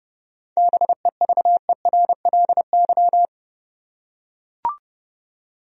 Morse code, heard as a keyed tone, about 30 wpm, 700 Hz. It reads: BEVERLY
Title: USA Common Names - 30wpm